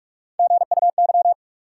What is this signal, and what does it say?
Morse code, heard as a keyed tone, about 46 wpm, 700 Hz. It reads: GUY
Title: Three Letter Words - 45wpm